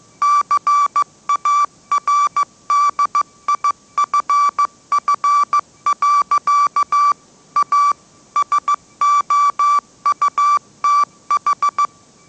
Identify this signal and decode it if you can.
Morse code, heard as a keyed tone, about 15 wpm, 1160 Hz. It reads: CARDIFF. A SOUTH
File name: mrx / mrx066 16wpm